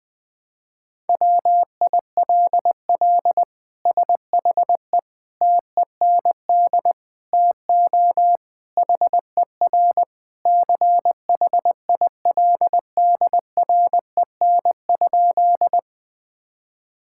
Morse code, heard as a keyed tone, about 20 wpm, 700 Hz. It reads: WILL SHE TEND TO HER CHILDREN?
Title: Sentences from Top 1000 Words - Review Part 5 - 20wpm